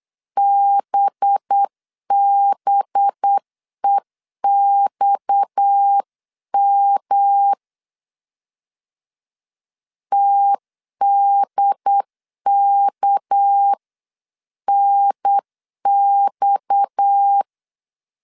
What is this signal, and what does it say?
Morse code, heard as a keyed tone, about 9 wpm, 800 Hz. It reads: BBEXM TDKNX
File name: CW / DET / 1 6 1500C S C 8wpm 5219